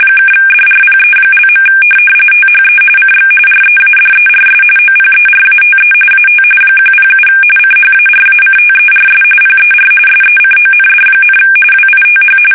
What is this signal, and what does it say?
RTTY, about 45.45 baud. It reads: (DRPE). THE CX READERS OPERATE AT 107 CHARACTERS PER SECOND, WITH
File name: fsk850 / teletype-corp